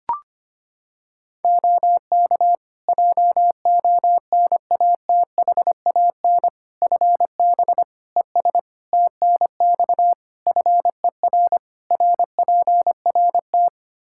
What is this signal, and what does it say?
Morse code, heard as a keyed tone, about 25 wpm, 700 Hz. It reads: OK JONATHAN FB ES TNX FER RPRT
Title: Practice QSO - Exchange 2 - Info - 25wpm